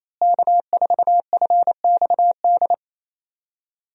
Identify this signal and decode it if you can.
Morse code, heard as a keyed tone, about 28 wpm, 700 Hz. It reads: K4FXD
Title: USA Call Signs - 5 Characters - Repeated 3x - 28wpm